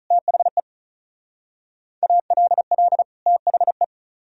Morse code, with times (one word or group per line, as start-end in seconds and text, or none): 0.1-0.6 THE
2.0-3.0 ALL
3.3-3.8 THE